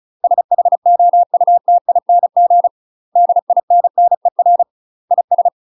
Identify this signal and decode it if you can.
Morse code, heard as a keyed tone, about 34 wpm, 700 Hz. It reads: SHOUTING DINNER IS
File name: UK MASTER QSOs TRAINING / 35WPM / UK MASTER QSOs 0039